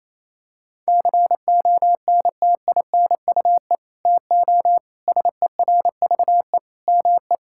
Morse code, heard as a keyed tone, about 28 wpm, 700 Hz. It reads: CONTINUE TO SERVE ME
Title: Sentences from Top 1000 Words - Review Part 6 - 28wpm